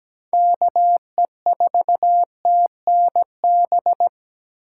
Morse code, heard as a keyed tone, about 17 wpm, 700 Hz. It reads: KE4TNB
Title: USA Call Signs - 6 Characters - 3x Character Spacing - 17wpm